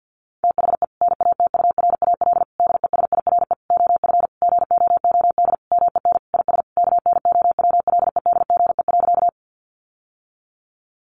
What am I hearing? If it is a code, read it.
Morse code, heard as a keyed tone, about 50 wpm, 700 Hz. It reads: THE NATURAL DESIRE OF GOOD MEN IS KNOWLEDGE.
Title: Sentences from Top 1100 Words 50wpm